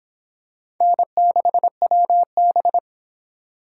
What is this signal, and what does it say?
Morse code, heard as a keyed tone, about 22 wpm, 700 Hz. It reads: N6WB